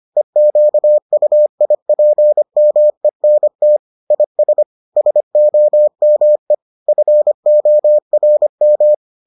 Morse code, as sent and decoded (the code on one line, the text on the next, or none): . --.- ..- .. .--. -- . -. - / .. ... / ... --- -- . / ..-. --- .-. --
EQUIPMENT IS SOME FORM